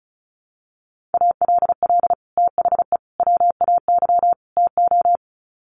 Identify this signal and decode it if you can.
Morse code, heard as a keyed tone, about 34 wpm, 700 Hz. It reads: ALL THE WAY TO
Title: Sets of 4 Words 35wpm